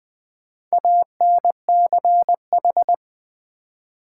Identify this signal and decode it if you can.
Morse code, heard as a keyed tone, about 20 wpm, 700 Hz. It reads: ANCH
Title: Build A Word - 2 Letters At A Time - 6-Letter Words - 20wpm